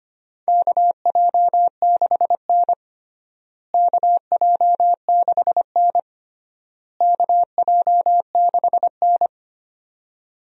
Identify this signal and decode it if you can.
Morse code, heard as a keyed tone, about 25 wpm, 700 Hz. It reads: KJ6N KJ6N KJ6N
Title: USA Call Signs - 4 Characters - Repeated 3x - 25wpm